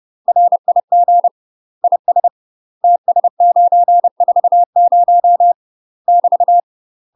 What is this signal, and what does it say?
Morse code, heard as a keyed tone, about 30 wpm, 700 Hz. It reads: RIG IS TS940 <BT>
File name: UK MASTER QSOs TRAINING / 30WPM / UK MASTER QSOs 0046